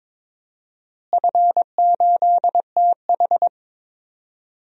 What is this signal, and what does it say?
Morse code, heard as a keyed tone, about 22 wpm, 700 Hz. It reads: F8TH
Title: DXCC-100 Call Signs - 4 Characters - Repeated 2x - 22wpm